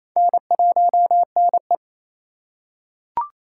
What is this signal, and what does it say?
Morse code, heard as a keyed tone, about 28 wpm, 700 Hz. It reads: N1NE
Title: USA Call Signs - 4 Characters - 2x Character Spacing - 28wpm